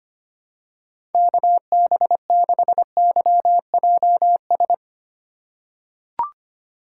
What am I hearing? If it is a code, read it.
Morse code, heard as a keyed tone, about 25 wpm, 700 Hz. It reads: KB6YJS
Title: USA Call Signs - 3x Character Spacing - 25wpm